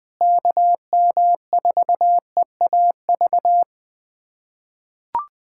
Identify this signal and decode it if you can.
Morse code, heard as a keyed tone, about 20 wpm, 700 Hz. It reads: KM4EAV
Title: USA Callsign DE Callsign 20wpm